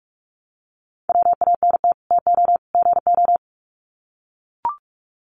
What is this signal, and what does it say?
Morse code, heard as a keyed tone, about 44 wpm, 700 Hz. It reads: WANT TO GO
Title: Sets of 3 Words 45wpm